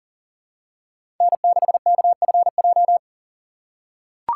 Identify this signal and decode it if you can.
Morse code, heard as a keyed tone, about 40 wpm, 700 Hz. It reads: N6KFJ
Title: USA Call Signs - 5 Characters - 6x Character Spacing - 40wpm